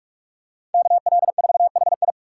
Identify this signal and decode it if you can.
Morse code, heard as a keyed tone, about 46 wpm, 700 Hz. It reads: KL4HI
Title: USA Call Signs - 5 Characters - 6x Character Spacing - 45wpm